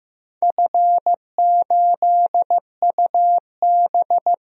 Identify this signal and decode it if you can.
Morse code, heard as a keyed tone, about 15 wpm, 700 Hz. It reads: F8UB
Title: DXCC-100 Call Signs - 4 Characters - Repeated 2x - 15wpm